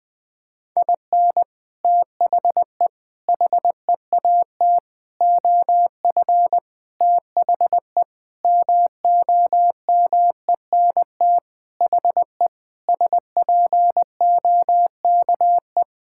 Morse code, heard as a keyed tone, about 20 wpm, 700 Hz. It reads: IN THE HEAT OF THE MOMENT HE SPOKE
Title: Sentences with 6-Word Phrases - 20wpm